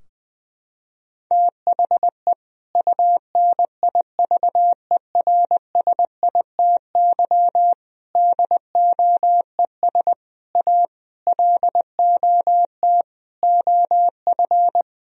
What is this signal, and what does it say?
Morse code, heard as a keyed tone, about 20 wpm, 700 Hz. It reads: THE UNIVERSITY DOES A LOT OF
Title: Sentences from Top 1000 Words - Review Part 4 - 20wpm